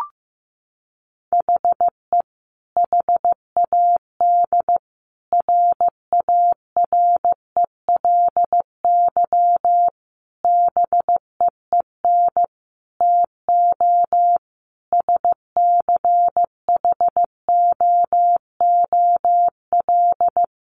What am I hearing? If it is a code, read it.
Morse code, heard as a keyed tone, about 15 wpm, 700 Hz. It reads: HE HAD RARELY BEEN TO SCHOOL
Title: Sentences from Top 1400 Words 15wpm